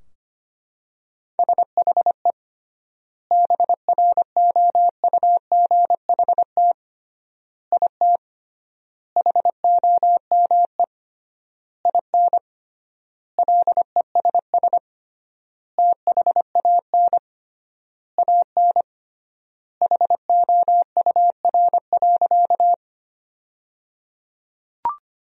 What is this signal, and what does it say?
Morse code, heard as a keyed tone, about 25 wpm, 700 Hz. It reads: SHE BROUGHT IT HOME IN LESS THAN AN HOUR.
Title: 3x Word-Spacing - Sentences from Top 500 Words - 25wpm